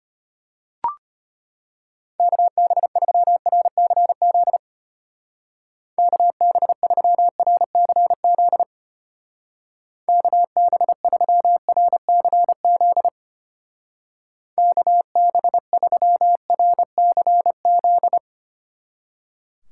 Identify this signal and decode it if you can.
Morse code, heard as a keyed tone, about 31 wpm, 700 Hz. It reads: KB3RCZ KB3RCZ KB3RCZ KB3RCZ
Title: Speed Racing - USA 2x3 Call Signs 25wpm